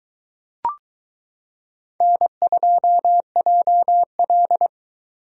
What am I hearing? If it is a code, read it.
Morse code, heard as a keyed tone, about 22 wpm, 700 Hz. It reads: N2JL